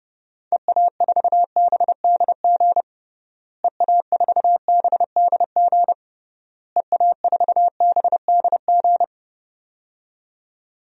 Morse code, heard as a keyed tone, about 30 wpm, 700 Hz. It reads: EA4BDG EA4BDG EA4BDG
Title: DXCC-100 Call Signs - 6 Characters - Repeated 3x - 30wpm